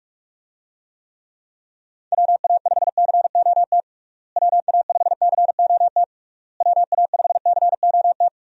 Morse code, heard as a keyed tone, about 46 wpm, 700 Hz. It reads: WA5COT WA5COT WA5COT
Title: USA Call Signs - 6 Characters - Repeated 3x - 45wpm